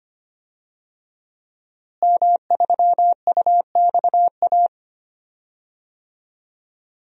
Morse code, heard as a keyed tone, about 25 wpm, 700 Hz. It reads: M3UXA
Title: DXCC-100 Call Signs - 5 Characters - Repeated 2x - 25wpm